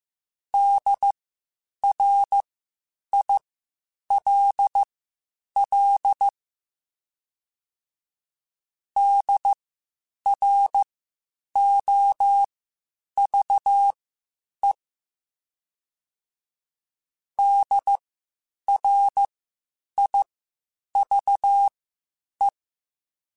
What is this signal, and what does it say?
Morse code, heard as a keyed tone, about 15 wpm, spaced as for 8 wpm, 800 Hz. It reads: DRILL DROVE DRIVE